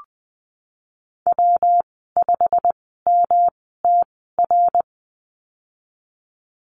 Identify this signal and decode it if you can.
Morse code, heard as a keyed tone, about 20 wpm, spaced as for 15 wpm, 700 Hz. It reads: W5MTR
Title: 161 Lesson 26g - Letter g - Call Signs 20wpm